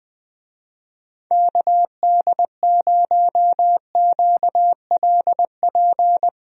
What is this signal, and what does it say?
Morse code, heard as a keyed tone, about 21 wpm, 700 Hz. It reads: KD0QLP
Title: USA Call Signs - 6 Characters - Repeated 3x - 20wpm